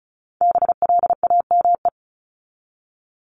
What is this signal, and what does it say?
Morse code, heard as a keyed tone, about 34 wpm, 700 Hz. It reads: BLAME
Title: Top 1751-2000 Words 35wpm